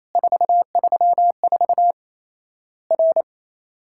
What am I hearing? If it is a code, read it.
Morse code, heard as a keyed tone, about 28 wpm, 700 Hz. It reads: 434 R
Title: Worked All Europe DX Contest - QTC Exchange - 28wpm